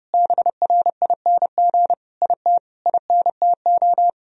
Morse code, heard as a keyed tone, about 32 wpm, 700 Hz. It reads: BRING IT INTO
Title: Sentences from Top 1000 Words - Review Part 4 - 30wpm